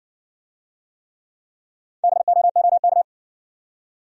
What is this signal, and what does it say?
Morse code, heard as a keyed tone, about 60 wpm, 700 Hz. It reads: BOOK